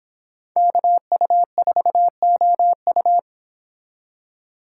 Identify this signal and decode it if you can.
Morse code, heard as a keyed tone, about 26 wpm, 700 Hz. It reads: KU4OU